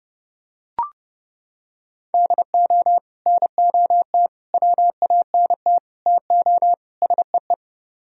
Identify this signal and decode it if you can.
Morse code, heard as a keyed tone, about 29 wpm, 700 Hz. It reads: DO NOT WANT TO SEE